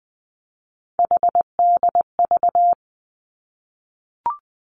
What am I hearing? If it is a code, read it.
Morse code, heard as a keyed tone, about 20 wpm, 700 Hz. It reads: HDV
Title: Three Challenging Letter-Number Combinations 20wpm